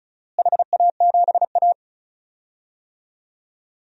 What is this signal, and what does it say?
Morse code, heard as a keyed tone, about 36 wpm, 700 Hz. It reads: HA7A
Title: DXCC-100 Call Signs - 4 Characters - Repeated 3x - 35wpm